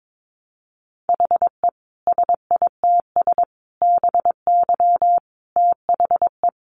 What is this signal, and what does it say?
Morse code, heard as a keyed tone, about 22 wpm, 700 Hz. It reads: HE SITS BY THE
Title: Additional Easy Phrases 22wpm